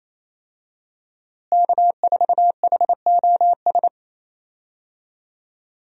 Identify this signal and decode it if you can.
Morse code, heard as a keyed tone, about 28 wpm, 700 Hz. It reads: K4HOS
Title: USA Call Signs - 5 Characters - Repeated 3x - 28wpm